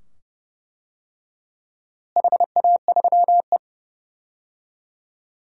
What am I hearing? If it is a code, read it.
Morse code, heard as a keyed tone, about 32 wpm, 700 Hz. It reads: HA3E
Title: DXCC-100 Call Signs - 4 Characters - Repeated 2x - 30wpm